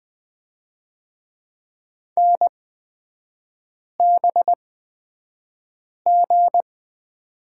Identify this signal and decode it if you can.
Morse code, heard as a keyed tone, about 20 wpm, spaced as for 5 wpm, 700 Hz. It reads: NBG
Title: 243 Lesson 39e - Letter z - Three Characters 20wpm